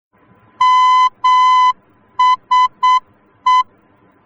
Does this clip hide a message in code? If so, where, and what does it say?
Yes: Morse code from 0.6 to 3.6 s, heard as a keyed tone, about 8 wpm, 1010 Hz. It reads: MSE